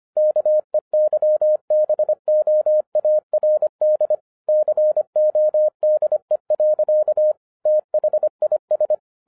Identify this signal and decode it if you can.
Morse code, heard as a keyed tone, about 24 wpm, 610 Hz. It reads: KEYBOARD CODE. THIS
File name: 2011-10-mp3 / n09-CW TO PS2 CONVERTER 25wpm